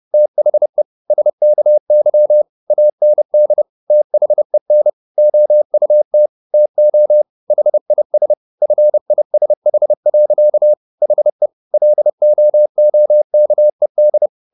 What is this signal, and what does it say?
Morse code, heard as a keyed tone, about 30 wpm, 600 Hz. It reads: THE SKY AND THEN OUT TO HIS FISH. HE LOOKED